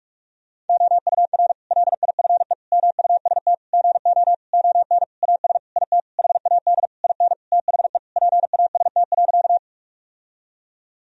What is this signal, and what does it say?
Morse code, heard as a keyed tone, about 46 wpm, 700 Hz. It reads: OUR LIFE MUST GO ON AS IT HAD IN THE PAST.
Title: Sentences from Top 1000 Words - Review Part 3 - 45wpm